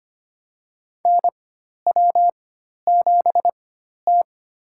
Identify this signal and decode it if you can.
Morse code, heard as a keyed tone, about 25 wpm, spaced as for 12 wpm, 700 Hz. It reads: NW7T
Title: USA Call Signs - 4 Characters - 4x Character Spacing - 25wpm